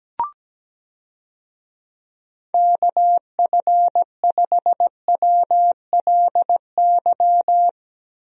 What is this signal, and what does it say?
Morse code, heard as a keyed tone, about 17 wpm, 700 Hz. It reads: KF5WLY
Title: USA Call Signs - 6 Characters - 17wpm